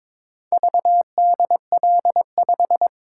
Morse code, heard as a keyed tone, about 22 wpm, 700 Hz. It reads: VDL5
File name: practice / 22wpm / Four Letter-Number Combinations - 22wpm